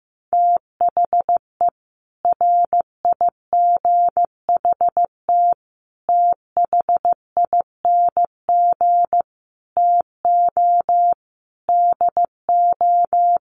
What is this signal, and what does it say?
Morse code, heard as a keyed tone, about 15 wpm, 700 Hz. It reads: THE RIGHT THING TO DO
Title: Sets of 5 Words 15wpm